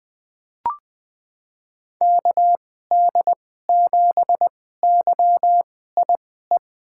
Morse code, heard as a keyed tone, about 20 wpm, spaced as for 14 wpm, 700 Hz. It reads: KD7YIE
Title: USA Call Signs - 2x Character Spacing - 20wpm